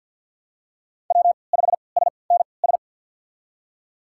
Morse code, heard as a keyed tone, about 52 wpm, spaced as for 29 wpm, 700 Hz. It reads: W5SNS
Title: USA Call Signs - 5 Characters - 3x Character Spacing - 50wpm